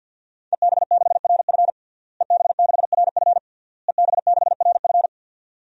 Morse code, heard as a keyed tone, about 48 wpm, 700 Hz. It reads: EB6RF EB6RF EB6RF
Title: DXCC-100 Call Signs - 5 Characters - Repeated 3x - 50wpm